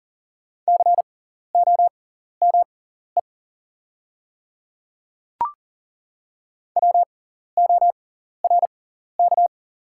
Morse code, heard as a keyed tone, about 40 wpm, spaced as for 14 wpm, 700 Hz. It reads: COME WORK
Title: 6x Character Spacing - Top 300 Words - 40wpm